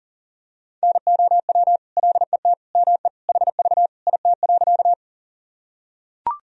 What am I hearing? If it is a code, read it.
Morse code, heard as a keyed tone, about 40 wpm, 700 Hz. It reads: NOW LET ME HV IT.
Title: Sentences from Top 300 Words and Abbreviations 40wpm